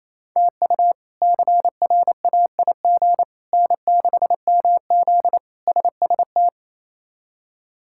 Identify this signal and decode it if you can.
Morse code, heard as a keyed tone, about 28 wpm, 700 Hz. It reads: TU CRAIG N6MZ SST
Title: K1USN SST QSO Practice - 28wpm